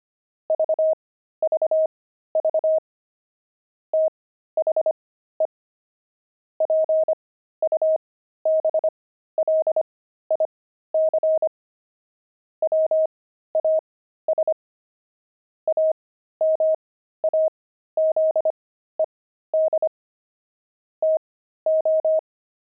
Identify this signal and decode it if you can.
Morse code, heard as a keyed tone, about 25 wpm, spaced as for 13 wpm, 630 Hz. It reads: VVV THE PUBLIC WAS AMAZED TO
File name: pangram-fs-13-cs-25-dexterity